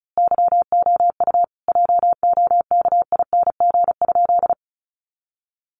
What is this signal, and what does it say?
Morse code, heard as a keyed tone, about 34 wpm, 700 Hz. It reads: YOU JOKING?
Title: A Sound Of Thunder 35wpm